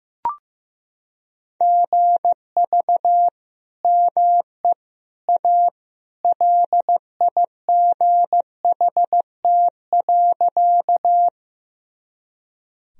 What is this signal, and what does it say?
Morse code, heard as a keyed tone, about 16 wpm, 700 Hz. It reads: GV ME A LIGHT.
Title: Sentences from Top 200 Words and Abbreviations 15wpm